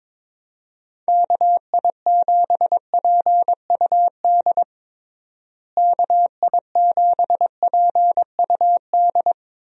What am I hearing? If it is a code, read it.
Morse code, heard as a keyed tone, about 21 wpm, 700 Hz. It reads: KI7PUD KI7PUD
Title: USA Call Signs - 6 Characters - Repeated 2x - 22wpm